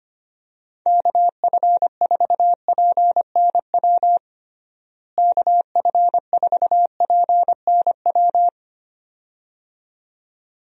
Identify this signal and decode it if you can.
Morse code, heard as a keyed tone, about 25 wpm, 700 Hz. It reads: KF4PNW KF4PNW
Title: USA Call Signs - 6 Characters - Repeated 2x - 25wpm